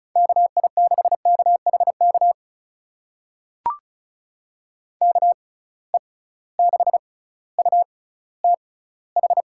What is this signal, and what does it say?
Morse code, heard as a keyed tone, about 36 wpm, spaced as for 18 wpm, 700 Hz. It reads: KI6KHK KE6UTH
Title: USA Call Signs - 6 Characters - 6x Character Spacing - 35wpm